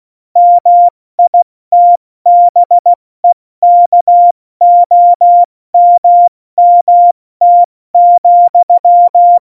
Morse code, heard as a keyed tone, about 16 wpm, spaced as for 14 wpm, 700 Hz. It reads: MITBEKOMMT,
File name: Morsekurs / Witze / Witz 11 Juli 17, Tempo 70 F80